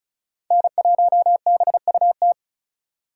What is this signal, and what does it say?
Morse code, heard as a keyed tone, about 36 wpm, 700 Hz. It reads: N1BUT
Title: USA Call Signs - 5 Characters - 2x Character Spacing - 35wpm